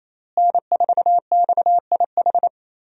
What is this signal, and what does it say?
Morse code, heard as a keyed tone, about 28 wpm, 700 Hz. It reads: N4XIH
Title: USA Call Signs - 5 Characters - Repeated 2x - 28wpm